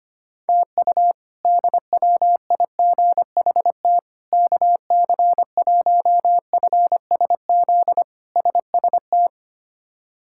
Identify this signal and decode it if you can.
Morse code, heard as a keyed tone, about 25 wpm, 700 Hz. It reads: TU DWIGHT KC1FSZ SST
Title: K1USN SST QSO Practice - 25wpm